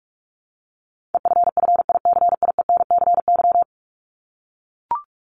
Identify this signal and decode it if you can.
Morse code, heard as a keyed tone, about 44 wpm, 700 Hz. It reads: EFFICIENCY
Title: Top 1501-1750 Words 45wpm